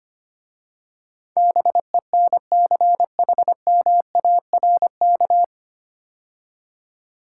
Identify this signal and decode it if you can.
Morse code, heard as a keyed tone, about 25 wpm, 700 Hz. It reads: BENCHMARK
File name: practice / 25wpm / 9 Letter Words - First 3 Letters Given - 25wpm